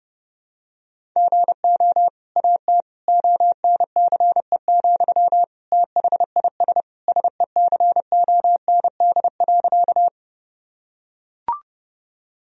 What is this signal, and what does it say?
Morse code, heard as a keyed tone, about 30 wpm, 700 Hz. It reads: GO AT ONCE, THIS SECOND.
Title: Sentences from Top 1000 Words - Review Part 1 - 30wpm